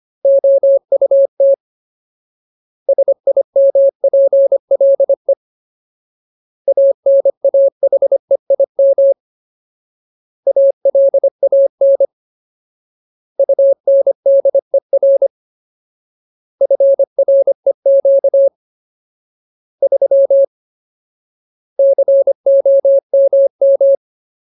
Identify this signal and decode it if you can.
Morse code, heard as a keyed tone, about 25 wpm, 550 Hz. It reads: OUT SIMPLE ANAHEIM ALAN UNDER FREQ 3 COMM
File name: Ultimate Word List / 25WPM Ultimate